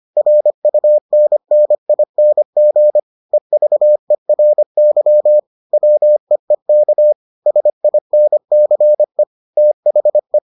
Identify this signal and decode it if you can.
Morse code, heard as a keyed tone, about 25 wpm, 600 Hz. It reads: RUNNING EVERY WEEK SINCE THE